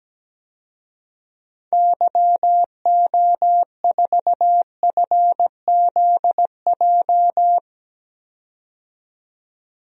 Morse code, heard as a keyed tone, about 17 wpm, 700 Hz. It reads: YO4FZJ
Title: DXCC-100 Call Signs - 6 Characters - 17wpm